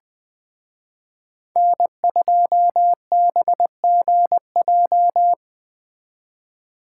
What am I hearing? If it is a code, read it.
Morse code, heard as a keyed tone, about 20 wpm, 700 Hz. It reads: N2BGJ
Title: Speed Racing - USA 1x3 Call Signs 17wpm